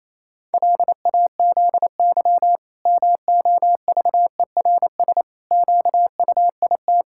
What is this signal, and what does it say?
Morse code, heard as a keyed tone, about 28 wpm, 700 Hz. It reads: LAZY MOVERS QUIT
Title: Pangram Challenge v2 - 28wpm